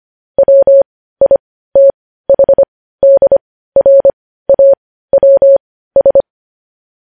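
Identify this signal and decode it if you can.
Morse code, heard as a keyed tone, about 25 wpm, spaced as for 15 wpm, 550 Hz. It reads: WITHDRAWS
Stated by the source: Morse news headlines at 15 wpm.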